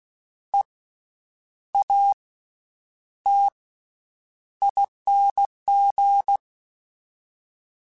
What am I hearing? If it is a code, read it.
Morse code, heard as a keyed tone, about 16 wpm, 800 Hz. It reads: E A T ING
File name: Special INGs-mp3s / Special INGs-mp3s / Slow ing @15